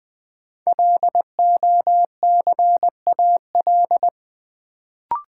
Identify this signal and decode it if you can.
Morse code, heard as a keyed tone, about 20 wpm, 700 Hz. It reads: LOCAL
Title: Sentences from Top 1500 Words - New Word Drill Alt - 20wpm